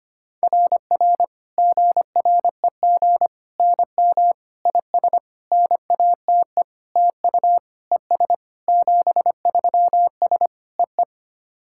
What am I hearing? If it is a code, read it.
Morse code, heard as a keyed tone, about 25 wpm, 700 Hz. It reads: RR GREG NM IS NATE TU ES 73S EE
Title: SOTA QSO Practice 25wpm